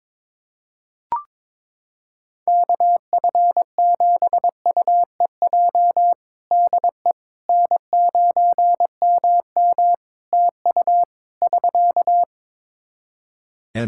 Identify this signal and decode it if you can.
Morse code, heard as a keyed tone, about 22 wpm, 700 Hz. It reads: KF7UEJ DE N9MM TU <SK>
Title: Practice QSO - Exchange 4 - Ending - 22wpm